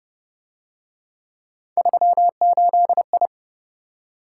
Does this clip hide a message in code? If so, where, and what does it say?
Yes: Morse code from 1.8 to 3.3 s, heard as a keyed tone, about 32 wpm, 700 Hz. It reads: 38I